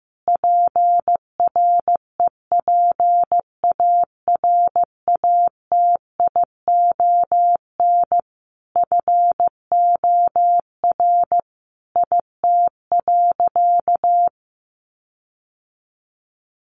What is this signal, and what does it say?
Morse code, heard as a keyed tone, about 15 wpm, 700 Hz. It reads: PREPARATION FOR IT.
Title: Sentences from Top 1400 Words 15wpm